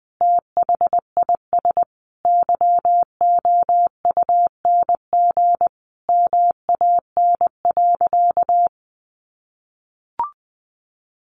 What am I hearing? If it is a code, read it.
Morse code, heard as a keyed tone, about 20 wpm, 700 Hz. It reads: THIS YOUNG MAN.
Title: Sentences from Top 1300 Words 20wpm